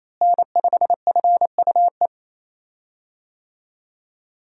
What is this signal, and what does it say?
Morse code, heard as a keyed tone, about 28 wpm, 700 Hz. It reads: N5FUE
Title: USA Call Signs - 5 Characters - Repeated 2x - 28wpm